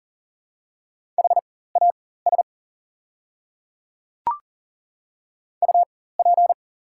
Morse code, heard as a keyed tone, about 41 wpm, spaced as for 19 wpm, 690 Hz. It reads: HAS UP
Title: 4x Character Spacing - Top 100 Words - 40wpm